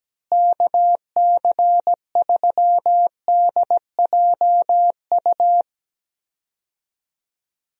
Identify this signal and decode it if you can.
Morse code, heard as a keyed tone, about 17 wpm, 700 Hz. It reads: KC3DJU
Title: USA Call Signs - 6 Characters - Repeated 3x - 17wpm